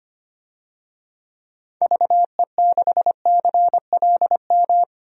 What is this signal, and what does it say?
Morse code, heard as a keyed tone, about 25 wpm, 700 Hz. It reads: VE6CLM
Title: DXCC-100 Call Signs - 6 Characters - 25wpm